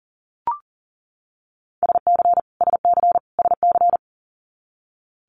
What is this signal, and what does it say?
Morse code, heard as a keyed tone, about 39 wpm, 700 Hz. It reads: SC SC SC
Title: Multi-letter Phonemes Repeated 3x 40wpm